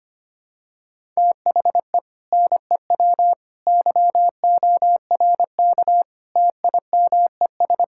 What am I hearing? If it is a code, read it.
Morse code, heard as a keyed tone, about 25 wpm, 700 Hz. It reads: THE NEW YORK TIMES
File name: practice / 25wpm / Sentences with 4-Word Phrases - 25wpm